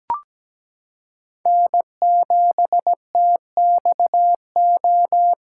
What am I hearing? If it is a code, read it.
Morse code, heard as a keyed tone, about 17 wpm, 700 Hz. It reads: N7TXO
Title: USA Callsign DE Callsign - 5 Characters - 17wpm